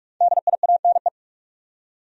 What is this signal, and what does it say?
Morse code, heard as a keyed tone, about 42 wpm, 700 Hz. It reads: DIANE
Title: USA Common Names - 45wpm